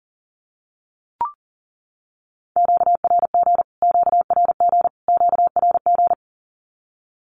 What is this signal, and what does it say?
Morse code, heard as a keyed tone, about 40 wpm, 700 Hz. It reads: QRG QRG QRG
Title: Q-Codes Repeated 3x 40wpm